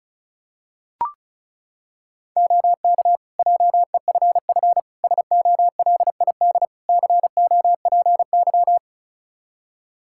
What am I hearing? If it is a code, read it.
Morse code, heard as a keyed tone, about 34 wpm, 700 Hz. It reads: OK JEFF SOLID COPY
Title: K1USN SST QSO Practice 35wpm